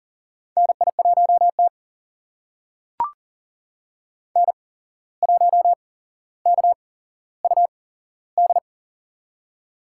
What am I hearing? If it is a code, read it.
Morse code, heard as a keyed tone, about 40 wpm, spaced as for 16 wpm, 700 Hz. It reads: NI1T N1KUD
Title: USA Call Signs - 8x Character Spacing - 40wpm